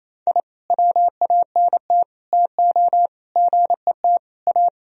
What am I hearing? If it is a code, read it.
Morse code, heard as a keyed tone, about 28 wpm, 700 Hz. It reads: I WANT TO GET A
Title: Sentences from Top 1000 Words - Review Part 1 - 28wpm